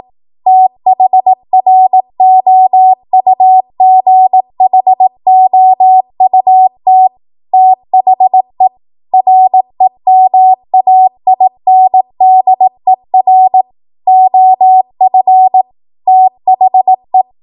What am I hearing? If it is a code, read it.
Morse code, heard as a keyed tone, about 18 wpm, 750 Hz. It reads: THROUGHOUT THE REMAINDER OF THE